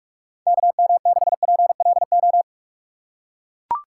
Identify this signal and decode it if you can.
Morse code, heard as a keyed tone, about 44 wpm, 700 Hz. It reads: KM6PLO
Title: USA Call Signs - 6 Characters - 6x Character Spacing - 45wpm